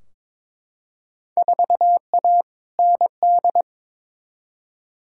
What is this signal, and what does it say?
Morse code, heard as a keyed tone, about 22 wpm, 700 Hz. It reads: 4A ND
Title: Field Day Exchanges Encore - 22wpm